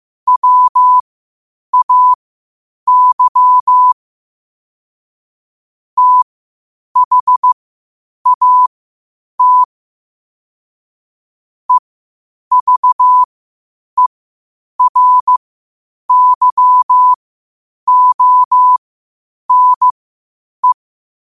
Morse code, heard as a keyed tone, about 15 wpm, spaced as for 8 wpm, 1000 Hz. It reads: WAY THAT EVERYONE
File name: qotdfiles / 8wpm